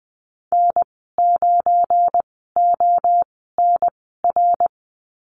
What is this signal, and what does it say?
Morse code, heard as a keyed tone, about 19 wpm, spaced as for 14 wpm, 700 Hz. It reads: N9ONR
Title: 231 Lesson 37g - Letter x - Call Signs 20wpm